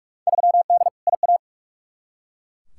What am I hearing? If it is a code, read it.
Morse code, heard as a keyed tone, about 42 wpm, 700 Hz. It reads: 3D IA
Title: Field Day Exchanges Encore - 45wpm